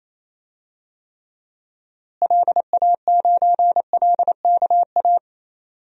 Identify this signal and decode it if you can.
Morse code, heard as a keyed tone, about 28 wpm, 700 Hz. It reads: LA9LKA
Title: DXCC-100 Call Signs - 6 Characters - Repeated 2x - 28wpm